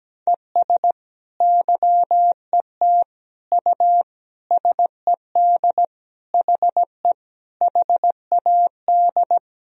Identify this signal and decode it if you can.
Morse code, heard as a keyed tone, about 17 wpm, 700 Hz. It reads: ES YET U SED HE HAD
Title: Sentences from Top 500 Words and Abbreviations 17wpm